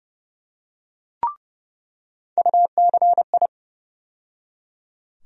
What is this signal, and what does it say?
Morse code, heard as a keyed tone, about 30 wpm, 700 Hz. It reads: UCI
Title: USA Call Sign Postfixes - 3 Letters 30wpm